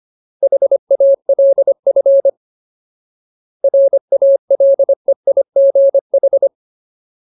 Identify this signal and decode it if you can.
Morse code, heard as a keyed tone, about 25 wpm, 550 Hz. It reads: HALF RALEIGH
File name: Ultimate Word List / 25WPM Ultimate